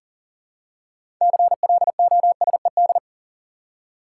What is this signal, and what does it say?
Morse code, heard as a keyed tone, about 39 wpm, 700 Hz. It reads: CLOSED
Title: Six Letter Words - 40wpm